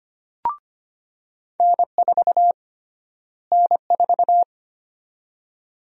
Morse code, heard as a keyed tone, about 25 wpm, 700 Hz. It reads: N4 N4
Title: USA Call Sign Prefixes - Repeated 2x - 25wpm